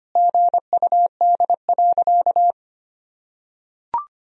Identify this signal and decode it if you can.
Morse code, heard as a keyed tone, about 25 wpm, 700 Hz. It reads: GUD.
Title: Sentences from Top 200 Words and Abbreviations 25wpm